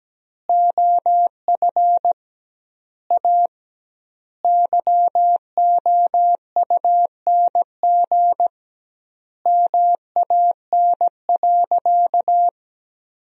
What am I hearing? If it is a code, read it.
Morse code, heard as a keyed tone, about 17 wpm, 700 Hz. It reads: OF A YOUNG MAN.
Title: 2x Word-Spacing - Sentences from Top 500 Words - 17wpm